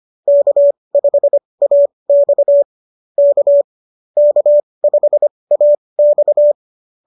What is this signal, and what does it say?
Morse code, heard as a keyed tone, about 25 wpm, spaced as for 20 wpm, 590 Hz. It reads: K5AX K K5AX
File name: qso201 20